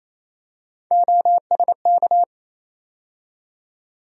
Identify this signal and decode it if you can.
Morse code, heard as a keyed tone, about 28 wpm, 700 Hz. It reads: OSK
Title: USA Call Sign Postfixes - 3 Letters 28wpm